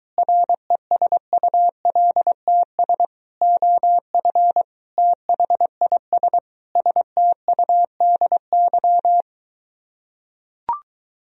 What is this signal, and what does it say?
Morse code, heard as a keyed tone, about 23 wpm, 700 Hz. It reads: RESULTS OF THIS STUDY